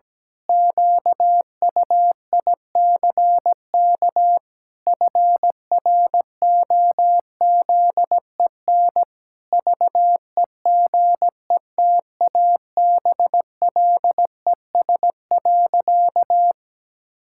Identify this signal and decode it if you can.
Morse code, heard as a keyed tone, about 17 wpm, 700 Hz. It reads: QUICK FROZEN VEGETABLES.
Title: Pangram Challenge v2 - 17wpm